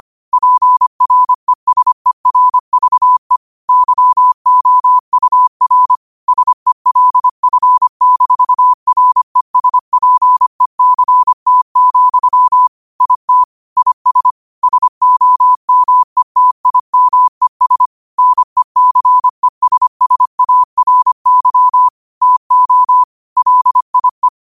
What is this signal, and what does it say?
Morse code, heard as a keyed tone, about 25 wpm, 1000 Hz. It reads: PRESERVE YOUR SELF-RESPECT, IT IS SOMETIMES NECESSARY TO LIE
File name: qotdfiles / 25wpm